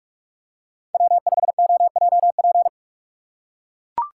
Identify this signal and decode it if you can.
Morse code, heard as a keyed tone, about 43 wpm, 700 Hz. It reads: W5OJP
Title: USA Call Signs - 5 Characters - 2x Character Spacing - 45wpm